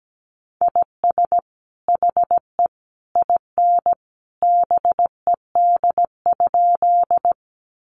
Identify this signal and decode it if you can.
Morse code, heard as a keyed tone, about 17 wpm, 700 Hz. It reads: IS HE IN BED?
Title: Sentences from Top 500 Words 17wpm